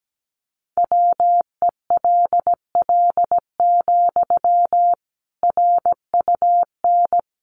Morse code, heard as a keyed tone, about 17 wpm, 700 Hz. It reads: WELL, RUN
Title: Sentences from Top 300 Words 17wpm